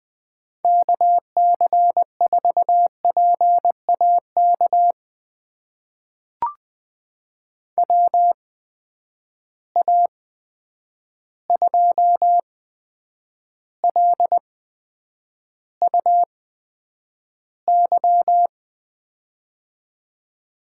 Morse code, heard as a keyed tone, about 20 wpm, spaced as for 9 wpm, 700 Hz. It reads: KC4PAK WA2LUY
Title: USA Call Signs - 6 Characters - 8x Character Spacing - 20wpm